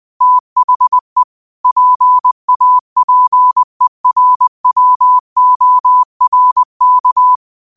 Morse code, heard as a keyed tone, about 20 wpm, 1000 Hz. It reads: THE PAPERWORK
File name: qotdfiles / 20wpm